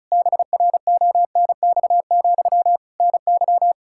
Morse code, heard as a keyed tone, about 36 wpm, 700 Hz. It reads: BRONX, NY
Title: Practice QSO - Exchange 1 - Beginning - 35wpm